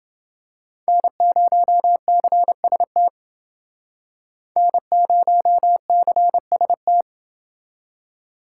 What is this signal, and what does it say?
Morse code, heard as a keyed tone, about 29 wpm, 700 Hz. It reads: N0CST N0CST